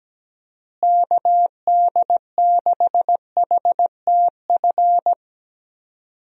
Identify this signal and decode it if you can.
Morse code, heard as a keyed tone, about 17 wpm, 700 Hz. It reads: KD6HTF
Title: USA Call Signs - 6 Characters - Repeated 2x - 17wpm